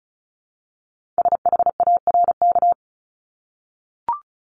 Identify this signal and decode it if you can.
Morse code, heard as a keyed tone, about 36 wpm, 700 Hz. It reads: SHARK
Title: Multi-letter Phonemes Repeated 3x 35wpm